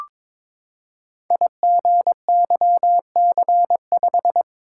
Morse code, heard as a keyed tone, about 22 wpm, 690 Hz. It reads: IGYC5
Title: Code Groups - 5 Letter:Numbers - 22wpm